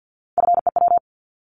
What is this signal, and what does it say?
Morse code, heard as a keyed tone, about 50 wpm, 690 Hz. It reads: FEW
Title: Multi-letter Phonemes 50wpm